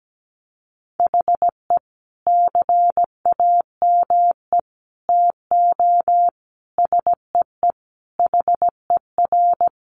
Morse code, heard as a keyed tone, about 17 wpm, 700 Hz. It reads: HE CAME TO SEE HER
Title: Sentences from Top 1300 Words 17wpm